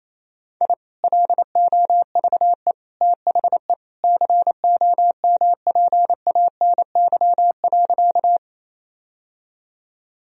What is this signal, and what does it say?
Morse code, heard as a keyed tone, about 28 wpm, 700 Hz. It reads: I LOVE THE COMPANY.
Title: Sentences from Top 1000 Words - Review Part 3 - 28wpm